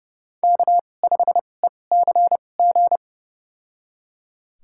Morse code, heard as a keyed tone, about 32 wpm, spaced as for 22 wpm, 700 Hz. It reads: K5ECG
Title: USA Call Signs - 5 Characters - 2x Character Spacing - 30wpm